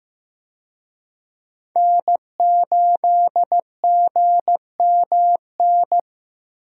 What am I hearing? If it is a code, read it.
Morse code, heard as a keyed tone, about 15 wpm, 700 Hz. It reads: N8GMN
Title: USA Call Signs - 5 Characters - 15wpm